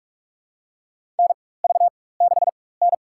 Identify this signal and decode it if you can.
Morse code, heard as a keyed tone, about 41 wpm, spaced as for 20 wpm, 700 Hz. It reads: NV6N
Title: USA Call Signs - 4x Character Spacing - 45wpm